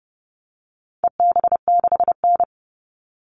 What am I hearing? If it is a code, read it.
Morse code, heard as a keyed tone, about 30 wpm, 700 Hz. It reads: EB6N
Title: All Cut Numbers - 4-Digit 30wpm